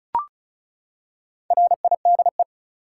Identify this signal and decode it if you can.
Morse code, heard as a keyed tone, about 36 wpm, 700 Hz. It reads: RIDE
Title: Four Letter Words - 35wpm